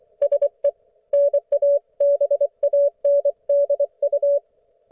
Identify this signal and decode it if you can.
Morse code, heard as a keyed tone, about 24 wpm, 570 Hz. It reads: SE NA BANDU
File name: Archiv / CUC224 / ok1fhi cul224